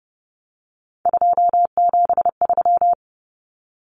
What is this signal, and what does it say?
Morse code, heard as a keyed tone, about 29 wpm, 700 Hz. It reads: 273
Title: Numbers - 3-Digit 30wpm